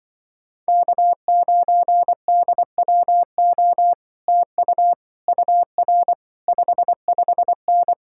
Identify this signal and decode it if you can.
Morse code, heard as a keyed tone, about 24 wpm, 700 Hz. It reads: K9DWO TU UR 55N
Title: Speed-Racing Common QSO Phrases - 20wpm